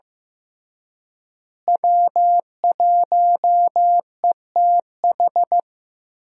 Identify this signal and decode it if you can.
Morse code, heard as a keyed tone, about 15 wpm, 700 Hz. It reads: W1ETH
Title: USA Call Signs - 5 Characters - Repeated 2x - 15wpm